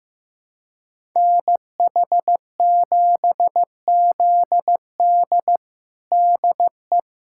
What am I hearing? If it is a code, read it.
Morse code, heard as a keyed tone, about 16 wpm, 700 Hz. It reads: NH7ZD DE
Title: USA Callsign DE Callsign - 5 Characters - 15wpm